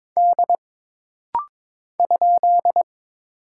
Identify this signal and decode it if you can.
Morse code, heard as a keyed tone, about 22 wpm, 700 Hz. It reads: D ?
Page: This rapid-fire timing is geared to push your ICR (Instant Character Recognition) skills.